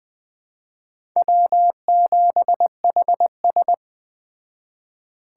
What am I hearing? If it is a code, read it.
Morse code, heard as a keyed tone, about 21 wpm, 700 Hz. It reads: W7HS
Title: Speed Racing - USA 1x2 Call Signs 17wpm